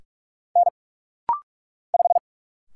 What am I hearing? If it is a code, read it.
Morse code, heard as a keyed tone, about 44 wpm, 700 Hz. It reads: N 5
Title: Single Letter-Number - Rapid-Fire 45wpm